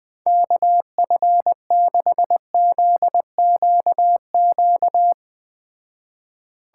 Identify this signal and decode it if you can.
Morse code, heard as a keyed tone, about 20 wpm, 700 Hz. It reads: KF6ZQQ